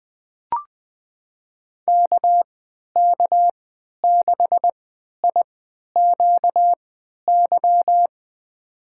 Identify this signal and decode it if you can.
Morse code, heard as a keyed tone, about 20 wpm, spaced as for 11 wpm, 700 Hz. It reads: KK6IQY
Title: USA Call Signs - 6 Characters - 3x Character Spacing - 20wpm